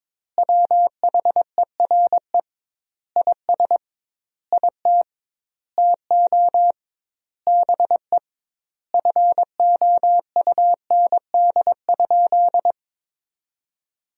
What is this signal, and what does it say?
Morse code, heard as a keyed tone, about 22 wpm, 700 Hz. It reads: WHERE IS IT TO BE FOUND?
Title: 2x Word-Spacing - Sentences from Top 300 Words - 22wpm